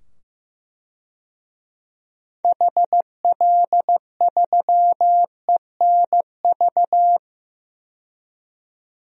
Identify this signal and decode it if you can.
Morse code, heard as a keyed tone, about 15 wpm, 700 Hz. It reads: HL3ENV
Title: DXCC-100 Call Signs - 6 Characters - 15wpm